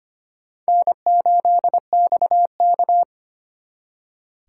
Morse code, heard as a keyed tone, about 31 wpm, 700 Hz. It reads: N8XK